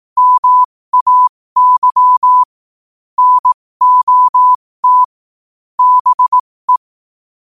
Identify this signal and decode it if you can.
Morse code, heard as a keyed tone, about 18 wpm, spaced as for 15 wpm, 1000 Hz. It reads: MAY NOT BE
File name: qotdfiles / 15wpm